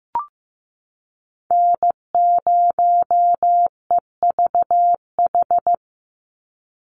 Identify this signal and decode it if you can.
Morse code, heard as a keyed tone, about 15 wpm, 700 Hz. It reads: N0EVH
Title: NA SOTA Chaser Callsigns 15wpm